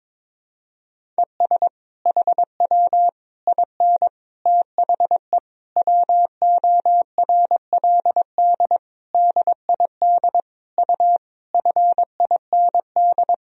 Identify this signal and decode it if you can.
Morse code, heard as a keyed tone, about 22 wpm, 700 Hz. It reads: ES HW IN THE WORLD DID U FIND
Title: Sentences from Top 200 Words and Abbreviations 22wpm